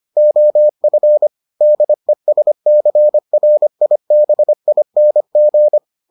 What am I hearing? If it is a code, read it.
Morse code, heard as a keyed tone, about 25 wpm, 600 Hz. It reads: OF DESCRIBING